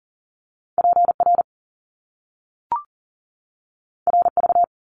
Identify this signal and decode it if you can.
Morse code, heard as a keyed tone, about 40 wpm, 700 Hz. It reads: PR RV
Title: Top Two Letter Combinations in Words 40wpm